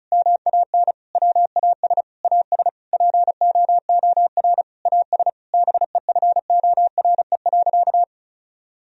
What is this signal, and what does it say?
Morse code, heard as a keyed tone, about 36 wpm, 700 Hz. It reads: MAN WAS AS POOR AS BEFORE.
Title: Sentences from Top 1000 Words - Review Part 2 - 35wpm